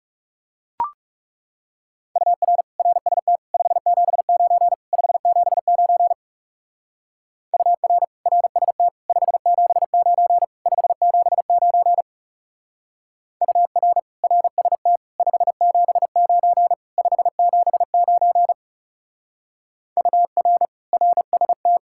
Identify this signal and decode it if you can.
Morse code, heard as a keyed tone, about 36 wpm, 700 Hz. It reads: UR RST 579 579 UR RST 579 579 UR RST 579 579 UR RST
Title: Speed-Racing Common QSO Phrases Encore - 30wpm